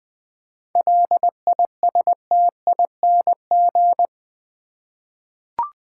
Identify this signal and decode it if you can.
Morse code, heard as a keyed tone, about 20 wpm, 700 Hz. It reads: LISTING
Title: 7 Letter Words - First 3 Letters Given - 20wpm